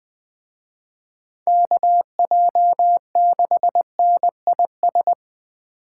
Morse code, heard as a keyed tone, about 20 wpm, 700 Hz. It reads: KJ6NIS
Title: Speed Racing - USA 2x3 Call Signs 17wpm